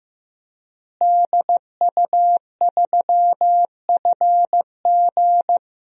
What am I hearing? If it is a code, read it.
Morse code, heard as a keyed tone, about 15 wpm, 700 Hz. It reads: DU3FG
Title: DXCC-100 Call Signs - 5 Characters - Repeated 2x - 15wpm